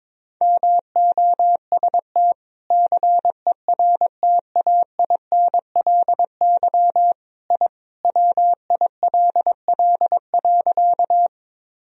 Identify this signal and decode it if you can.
Morse code, heard as a keyed tone, about 21 wpm, 700 Hz. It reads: MOST CERTAINLY I WILL.
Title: Sentences from Top 1000 Words - Review Part 1 - 22wpm